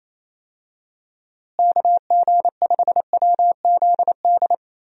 Morse code, heard as a keyed tone, about 28 wpm, 700 Hz. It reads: KG5WZD
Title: USA Call Signs - 6 Characters - Repeated 2x - 28wpm